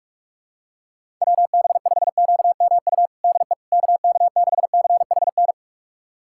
Morse code, heard as a keyed tone, about 41 wpm, 700 Hz. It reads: WB5QMU DE KK6CHN
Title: USA Callsign DE Callsign 45wpm